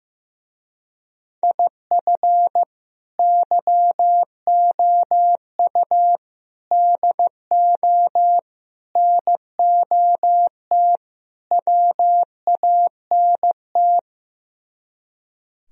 Morse code, heard as a keyed tone, about 15 wpm, 700 Hz. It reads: IF YOU DO NOT WANT
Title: Speed-Racing - Sets of 5 Words 15wpm